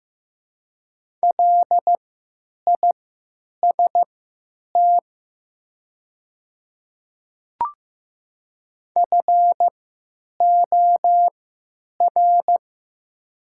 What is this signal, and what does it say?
Morse code, heard as a keyed tone, about 15 wpm, spaced as for 9 wpm, 700 Hz. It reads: LIST FOR
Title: 3x Character Spacing - Top 500 Words - 15wpm